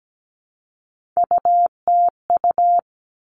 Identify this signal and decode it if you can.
Morse code, heard as a keyed tone, about 17 wpm, 700 Hz. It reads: UTU
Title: All Cut Numbers - 3-Digit 17wpm